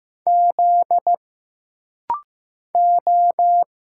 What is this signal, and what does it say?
Morse code, heard as a keyed tone, about 15 wpm, 700 Hz. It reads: Z O
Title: Single Letter-Number Alt - Rapid-Fire 15wpm